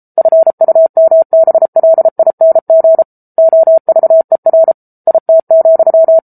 Morse code, heard as a keyed tone, about 34 wpm, 650 Hz. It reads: FUMBLING OVER IT,
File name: Book Two / 07-Chapter 7 - 33 WPM